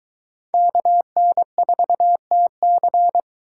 Morse code, heard as a keyed tone, about 22 wpm, 700 Hz. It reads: KN4TC